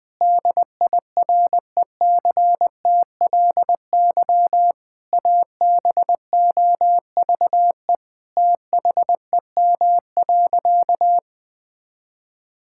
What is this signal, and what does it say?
Morse code, heard as a keyed tone, about 20 wpm, 700 Hz. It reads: DIRECTLY ABOVE THEM.
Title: Sentences from Top 1000 Words - Review Part 5 - 20wpm